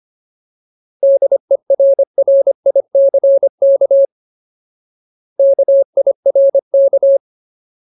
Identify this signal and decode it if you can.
Morse code, heard as a keyed tone, about 25 wpm, 550 Hz. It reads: DERRICK KIRK
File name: Ultimate Word List / 25WPM Ultimate